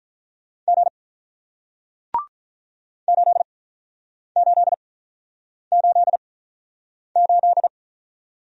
Morse code, heard as a keyed tone, about 48 wpm, spaced as for 9 wpm, 700 Hz. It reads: G 8888